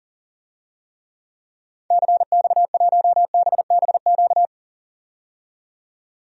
Morse code, heard as a keyed tone, about 40 wpm, 700 Hz. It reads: CX1BBQ
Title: DXCC-100 Call Signs - 6 Characters - 40wpm